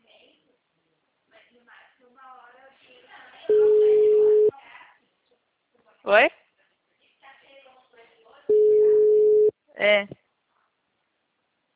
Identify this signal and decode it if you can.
Ringback tone (Europe, 425 Hz), with speech.
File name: out- -2005- - - .22745